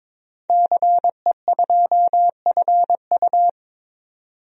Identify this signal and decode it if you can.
Morse code, heard as a keyed tone, about 22 wpm, 700 Hz. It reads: CE2FU
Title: DXCC-100 Call Signs - 5 Characters - 22wpm